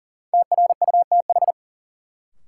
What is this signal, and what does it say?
Morse code, heard as a keyed tone, about 40 wpm, 700 Hz. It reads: TRUTH
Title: Sentences from Top 800 Words - New Word Drill - 40wpm